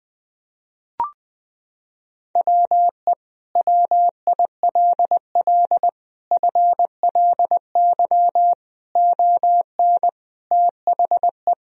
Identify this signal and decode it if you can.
Morse code, heard as a keyed tone, about 21 wpm, 700 Hz. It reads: WE WILL FLY ON THE
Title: Sentences from Top 1000 Words - Review Part 3 - 20wpm